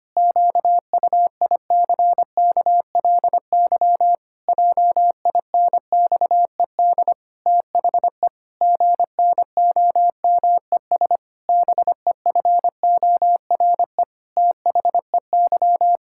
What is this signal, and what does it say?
Morse code, heard as a keyed tone, about 25 wpm, 700 Hz. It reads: QUICKLY JINXED THE GNOMES BEFORE THEY
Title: Pangram Challenge v2 - 25wpm